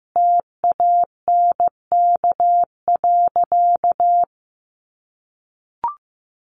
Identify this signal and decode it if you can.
Morse code, heard as a keyed tone, about 15 wpm, 700 Hz. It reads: TANK.
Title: Additional Easy Phrases 15wpm